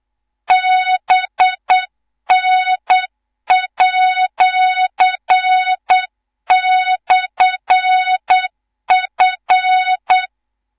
Morse code, heard as a keyed tone, about 8 wpm, 3700 Hz, with noise. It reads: BN@/F
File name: Morze-HA2MN / hang / 18 IRASJELEK2